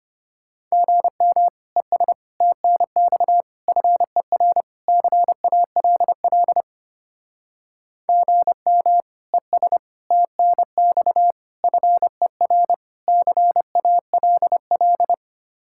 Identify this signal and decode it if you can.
Morse code, heard as a keyed tone, about 31 wpm, 700 Hz. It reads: GM ES TNX FER CALL GM ES TNX FER CALL
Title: Speed-Racing Common QSO Phrases - 25wpm